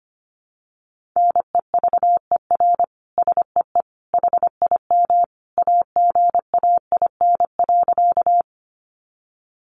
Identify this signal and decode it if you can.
Morse code, heard as a keyed tone, about 25 wpm, 700 Hz. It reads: NEVER SEE HIM AGAIN.
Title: Sentences from Top 300 Words 25wpm